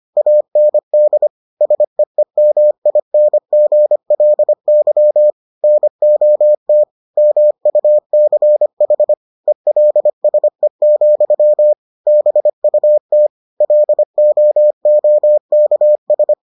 Morse code, heard as a keyed tone, about 25 wpm, 600 Hz. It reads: AND SEEMINGLY NOT MUCH ELSE, BUT LOOKS